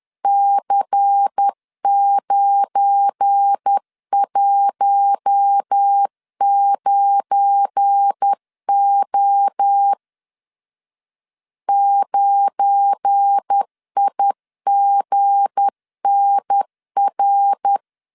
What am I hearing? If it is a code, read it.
Morse code, heard as a keyed tone, about 10 wpm, 800 Hz. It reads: C919O 9IGNR
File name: LEZIONI / 7 GF9Z / 7 GF9 10wpm 2747